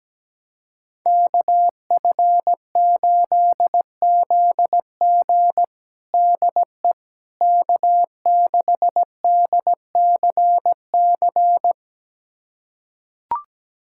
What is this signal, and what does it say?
Morse code, heard as a keyed tone, about 17 wpm, 700 Hz. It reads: KF8ZG DE K6DCC
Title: USA Callsign DE Callsign - 5 Characters - 17wpm